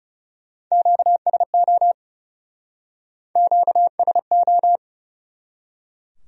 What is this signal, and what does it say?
Morse code, heard as a keyed tone, about 39 wpm, 700 Hz. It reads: QSO QSO